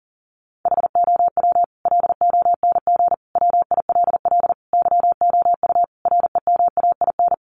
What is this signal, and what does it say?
Morse code, heard as a keyed tone, about 40 wpm, 700 Hz. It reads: HOW LONG WILL YOU REMAIN
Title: Sentences from Top 1200 Words 40wpm